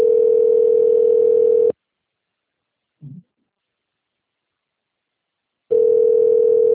Ringback tone (North America), 440 + 480 Hz.